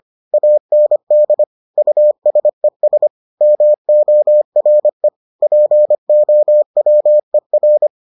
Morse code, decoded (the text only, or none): AND USES MORE POWER